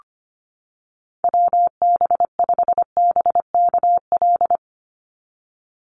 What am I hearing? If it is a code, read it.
Morse code, heard as a keyed tone, about 25 wpm, 700 Hz. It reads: WB5BKL
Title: Call Signs - 25wpm